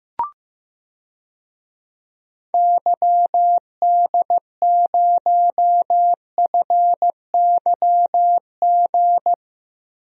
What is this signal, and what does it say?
Morse code, heard as a keyed tone, about 15 wpm, 700 Hz. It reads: YD0FYG
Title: DXCC-100 Call Signs - 6 Characters - Repeated 2x - 15wpm